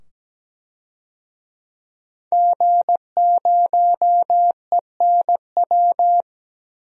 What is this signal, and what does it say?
Morse code, heard as a keyed tone, about 17 wpm, 700 Hz. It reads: G0ENW
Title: DXCC-100 Call Signs - 5 Characters - 17wpm